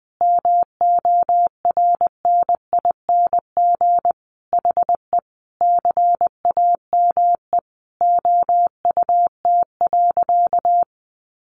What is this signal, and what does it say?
Morse code, heard as a keyed tone, about 20 wpm, 700 Hz. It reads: MORNING HE CAME OUT.
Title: Sentences from Top 1300 Words 20wpm